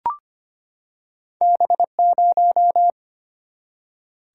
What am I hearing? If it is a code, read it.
Morse code, heard as a keyed tone, about 25 wpm, 700 Hz. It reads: B0
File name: Two Letter-Number Combinations - 25wpm